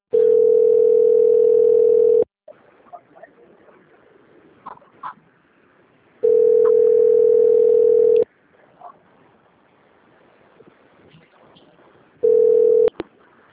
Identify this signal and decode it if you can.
Ringback tone (North America), 440 + 480 Hz.